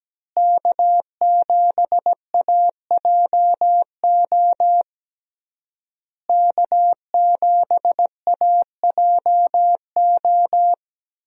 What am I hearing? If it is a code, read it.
Morse code, heard as a keyed tone, about 17 wpm, 700 Hz. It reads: K7AJO K7AJO
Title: USA Call Signs - 5 Characters - Repeated 3x - 17wpm